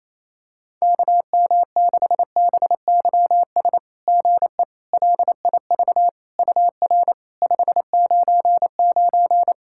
Morse code, heard as a keyed tone, about 28 wpm, 690 Hz. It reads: KM6BYS GE LIV UR 599
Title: SOTA QSO Practice 28wpm